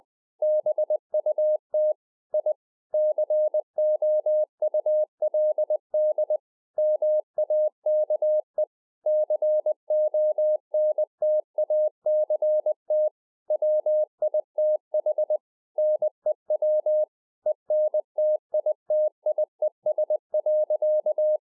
Morse code, heard as a keyed tone, about 20 wpm, 610 Hz. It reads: BUT I COULD MAKE CONTACT WITH NEW ENTITIES.